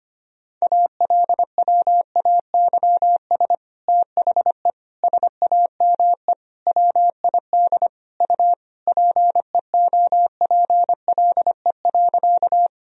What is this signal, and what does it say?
Morse code, heard as a keyed tone, about 25 wpm, 700 Hz. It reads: ALWAYS THE SAME WID U PEOPLE.
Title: Sentences from Top 300 Words and Abbreviations 25wpm